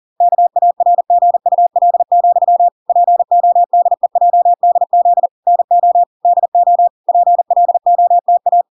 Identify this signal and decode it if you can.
Morse code, heard as a keyed tone, about 40 wpm, 700 Hz. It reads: KARGUL, PODEJDZ NO DO PLOTA